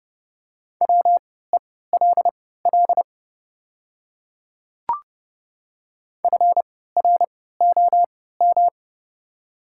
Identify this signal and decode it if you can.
Morse code, heard as a keyed tone, about 30 wpm, spaced as for 17 wpm, 700 Hz. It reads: WELL FROM
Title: 3x Character Spacing - Top 300 Words - 30wpm